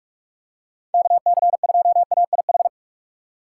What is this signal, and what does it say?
Morse code, heard as a keyed tone, about 44 wpm, 700 Hz. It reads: KC2AIH
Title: USA Call Signs - 6 Characters - 6x Character Spacing - 45wpm